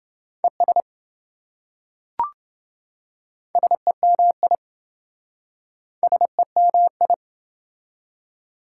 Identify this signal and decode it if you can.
Morse code, heard as a keyed tone, about 26 wpm, 700 Hz. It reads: ES SEMI SEMI